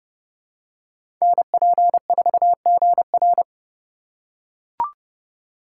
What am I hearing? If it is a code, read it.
Morse code, heard as a keyed tone, about 30 wpm, 700 Hz. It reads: NP4GR
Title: USA Call Signs - 2x Character Spacing - 30wpm